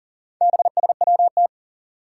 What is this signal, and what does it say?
Morse code, heard as a keyed tone, about 42 wpm, 700 Hz. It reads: BSWT
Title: Four Challenging Letter-Number Combinations 40wpm